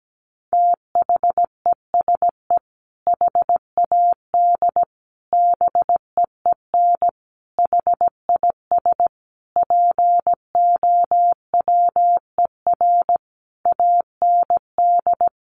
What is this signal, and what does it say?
Morse code, heard as a keyed tone, about 17 wpm, 700 Hz. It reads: THESE HAD BEEN HIS POWER AND
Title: Sentences from Top 1500 Words 17wpm